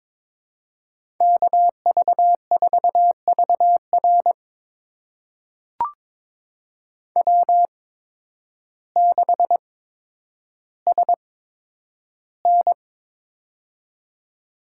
Morse code, heard as a keyed tone, about 22 wpm, spaced as for 10 wpm, 700 Hz. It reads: KV4VR W6SN
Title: USA Call Signs - 8x Character Spacing - 22wpm